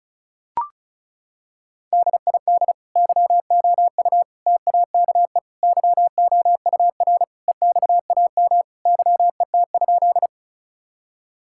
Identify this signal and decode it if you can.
Morse code, heard as a keyed tone, about 36 wpm, 700 Hz. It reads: DID YOU TAKE YOUR EXAM YET?
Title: Sentences from Top 1000 Words - Review Part 5 - 35wpm